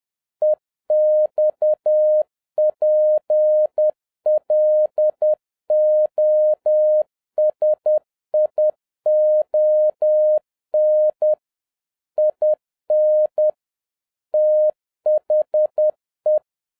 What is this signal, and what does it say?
Morse code, heard as a keyed tone, about 10 wpm, 610 Hz. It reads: EXPLOSION IN THE
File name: 2011-07-mp3 / n36 THE 3.11 2 10wpm